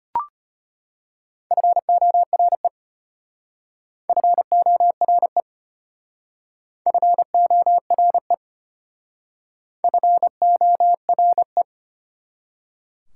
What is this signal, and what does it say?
Morse code, heard as a keyed tone, about 31 wpm, 690 Hz. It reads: FORE FORE FORE FORE
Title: Speed-Racing Prefixes and Suffixes 25wpm